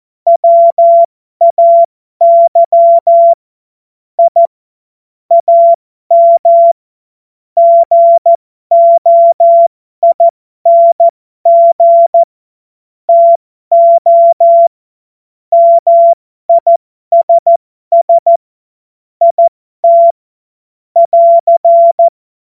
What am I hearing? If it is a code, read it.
Morse code, heard as a keyed tone, about 14 wpm, spaced as for 12 wpm, 680 Hz. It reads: WAY I AM GOING TO MISS IT <AR>